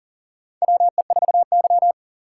Morse code, heard as a keyed tone, about 38 wpm, 700 Hz. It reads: WE4Y
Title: USA Call Signs - 3x Character Spacing - 40wpm